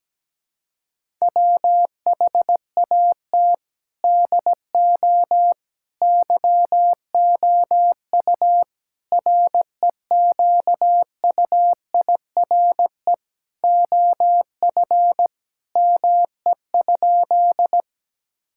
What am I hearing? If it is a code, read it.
Morse code, heard as a keyed tone, about 17 wpm, 700 Hz. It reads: WHAT DO YOU REQUIRE OF ME?
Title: Sentences from Top 1000 Words - Review Part 6 - 17wpm